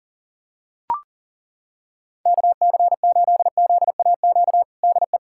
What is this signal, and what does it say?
Morse code, heard as a keyed tone, about 40 wpm, 700 Hz. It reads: KC8ZAQ DE
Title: USA Callsign DE Callsign 40wpm